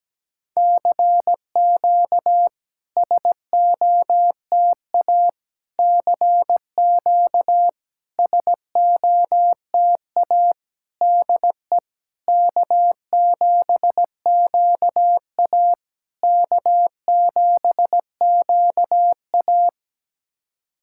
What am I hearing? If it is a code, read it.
Morse code, heard as a keyed tone, about 17 wpm, 690 Hz. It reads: CQ SOTA CQ SOTA DE K7QA K7QA
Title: SOTA QSO Practice 17wpm